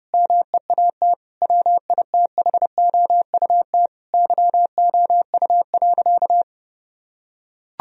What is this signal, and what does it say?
Morse code, heard as a keyed tone, about 30 wpm, 700 Hz. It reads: MEAT WITHOUT YOU.
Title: Sentences from Top 1000 Words - Review Part 3 - 30wpm